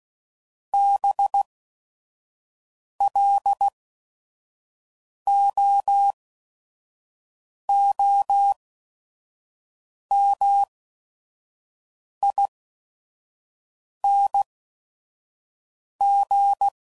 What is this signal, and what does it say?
Morse code, heard as a keyed tone, about 16 wpm, spaced as for 5 wpm, 800 Hz. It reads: BLOOMING
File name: Suffixes / Special ING's / VerySlowToFaster ING's